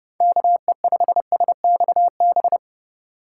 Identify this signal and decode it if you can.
Morse code, heard as a keyed tone, about 29 wpm, 700 Hz. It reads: KE5SXB
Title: USA Call Signs - 6 Characters - 2x Character Spacing - 30wpm